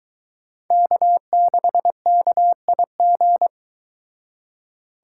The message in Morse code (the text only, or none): K6KIG